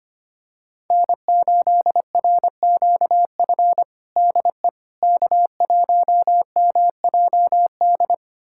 Morse code, heard as a keyed tone, about 25 wpm, 700 Hz. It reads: N8RQF DE K1MJD
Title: USA Callsign DE Callsign - 5 Characters - 25wpm